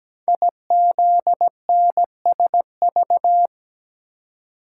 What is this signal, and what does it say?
Morse code, heard as a keyed tone, about 17 wpm, 700 Hz. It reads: IZNSV
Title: Code Groups - 5 Letters - 17wpm